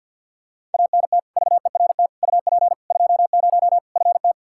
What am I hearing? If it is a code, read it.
Morse code, heard as a keyed tone, about 52 wpm, 700 Hz. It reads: ANT VERT UP 20 FT
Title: Practice QSO - Exchange 2 - Info - 50wpm